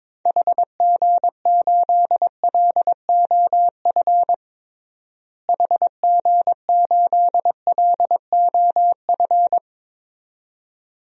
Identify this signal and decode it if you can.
Morse code, heard as a keyed tone, about 22 wpm, 700 Hz. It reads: HG8LOF HG8LOF
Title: DXCC-100 Call Signs - 6 Characters - Repeated 2x - 22wpm